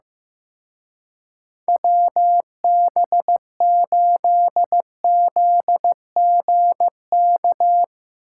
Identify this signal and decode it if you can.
Morse code, heard as a keyed tone, about 15 wpm, 700 Hz. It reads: WB8ZGK
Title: Speed Racing - USA 2x3 Call Signs 15wpm